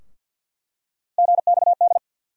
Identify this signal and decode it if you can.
Morse code, heard as a keyed tone, about 49 wpm, 690 Hz. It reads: GXD
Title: USA Call Sign Postfixes - 3 Letters 50wpm